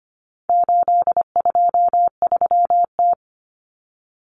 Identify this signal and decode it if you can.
Morse code, heard as a keyed tone, about 25 wpm, 700 Hz. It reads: 823T
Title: Common Cut Numbers - 4-Digit 25wpm